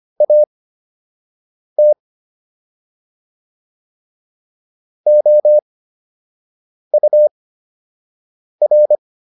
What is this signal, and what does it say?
Morse code, heard as a keyed tone, about 25 wpm, spaced as for 6 wpm, 600 Hz. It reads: AT OUR